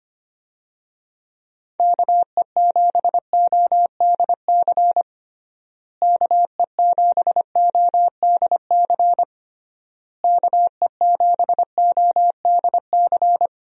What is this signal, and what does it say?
Morse code, heard as a keyed tone, about 25 wpm, 700 Hz. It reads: KE7ODC KE7ODC KE7ODC
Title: USA Call Signs - 6 Characters - Repeated 3x - 25wpm